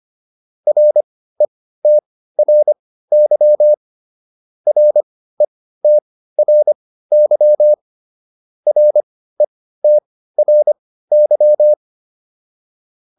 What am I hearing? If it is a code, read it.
Morse code, heard as a keyed tone, about 25 wpm, spaced as for 15 wpm, 600 Hz. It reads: RETRY RETRY RETRY